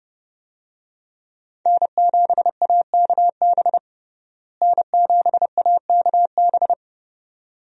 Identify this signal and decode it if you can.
Morse code, heard as a keyed tone, about 32 wpm, 700 Hz. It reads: N7AKB N7AKB
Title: USA Call Signs - 5 Characters - Repeated 2x - 30wpm